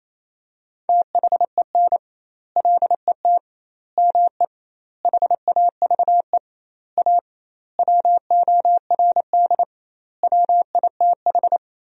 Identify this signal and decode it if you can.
Morse code, heard as a keyed tone, about 28 wpm, 700 Hz. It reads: THEN LET ME HAVE A WORD WITH
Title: 2x Word-Spacing - Sentences from Top 300 Words - 28wpm